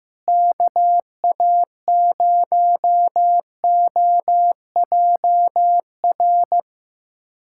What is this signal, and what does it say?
Morse code, heard as a keyed tone, about 15 wpm, 700 Hz. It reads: KA0OJR
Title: USA Callsign DE Callsign 15wpm